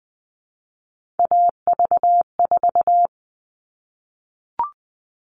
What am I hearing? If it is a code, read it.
Morse code, heard as a keyed tone, about 20 wpm, 700 Hz. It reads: AV4
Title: All Cut Numbers - 3-Digit 20wpm